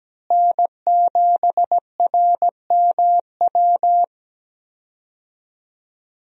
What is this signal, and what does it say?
Morse code, heard as a keyed tone, about 17 wpm, 700 Hz. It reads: N7RMW
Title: USA Call Signs - 5 Characters - Repeated 3x - 17wpm